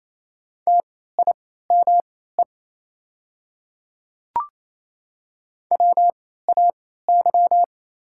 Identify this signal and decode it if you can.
Morse code, heard as a keyed tone, about 27 wpm, spaced as for 16 wpm, 690 Hz. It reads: TIME WAY
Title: 3x Character Spacing - Top 100 Words - 28wpm